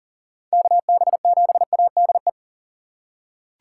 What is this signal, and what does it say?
Morse code, heard as a keyed tone, about 40 wpm, 700 Hz. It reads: KB7ADE
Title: USA Call Signs - 6 Characters - 6x Character Spacing - 40wpm